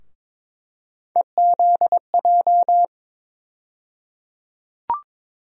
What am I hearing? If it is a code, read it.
Morse code, heard as a keyed tone, about 22 wpm, 700 Hz. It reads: EZJ
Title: USA Call Sign Postfixes - 3 Letters 22wpm